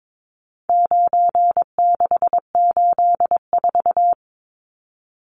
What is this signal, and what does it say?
Morse code, heard as a keyed tone, about 22 wpm, 700 Hz. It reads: 9684
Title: Numbers - 4-Digit 22wpm